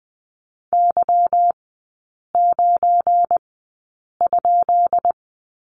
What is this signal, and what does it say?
Morse code, heard as a keyed tone, about 20 wpm, spaced as for 8 wpm, 700 Hz. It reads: Y9?
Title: 172 Lesson 28e - Number 9 - Three Characters 20wpm